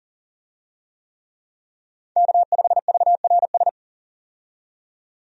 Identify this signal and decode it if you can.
Morse code, heard as a keyed tone, about 40 wpm, 700 Hz. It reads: K5VRS
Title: USA Call Signs - 5 Characters - 40wpm